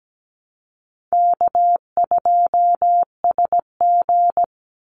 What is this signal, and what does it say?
Morse code, heard as a keyed tone, about 17 wpm, 700 Hz. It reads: K2SG
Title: Easy Call Signs - 17wpm